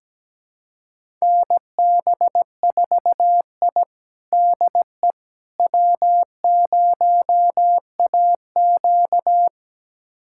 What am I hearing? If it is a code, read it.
Morse code, heard as a keyed tone, about 17 wpm, 700 Hz. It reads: NB4I DE W0AQ
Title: USA Callsign DE Callsign - 4 Characters - 17wpm